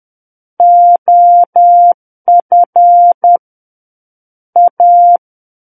Morse code, heard as a keyed tone, about 10 wpm, 700 Hz. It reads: OF A
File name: Book One / 01-Chapter 1 - 10 WPM